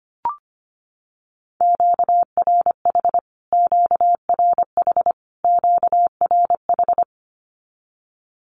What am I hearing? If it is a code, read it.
Morse code, heard as a keyed tone, about 25 wpm, 700 Hz. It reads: QRH QRH QRH
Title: Q-Codes Repeated 3x 25wpm